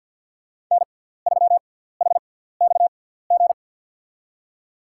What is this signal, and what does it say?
Morse code, heard as a keyed tone, about 47 wpm, spaced as for 17 wpm, 700 Hz. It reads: N3HXG
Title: USA Call Signs - 5 Characters - 6x Character Spacing - 50wpm